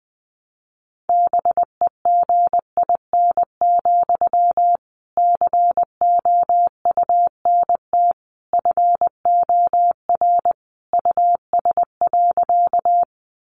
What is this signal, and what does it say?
Morse code, heard as a keyed tone, about 20 wpm, 700 Hz. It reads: BEGIN, COUNT FOR US.
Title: Sentences from Top 1500 Words 20wpm